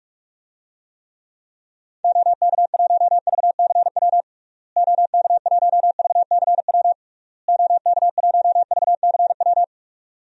Morse code, heard as a keyed tone, about 46 wpm, 700 Hz. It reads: OK1VCW OK1VCW OK1VCW
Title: DXCC-100 Call Signs - 6 Characters - Repeated 3x - 45wpm